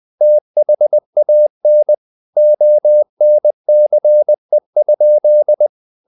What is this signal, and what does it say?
Morse code, heard as a keyed tone, about 20 wpm, 600 Hz. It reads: THAN ONCE?
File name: the old man and the sea / the old man and the sea part 1 20wpm